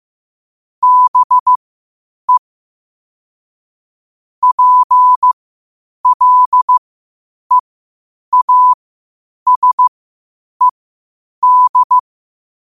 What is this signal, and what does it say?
Morse code, heard as a keyed tone, about 15 wpm, spaced as for 8 wpm, 1000 Hz. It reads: BE PLEASED
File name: qotdfiles / 8wpm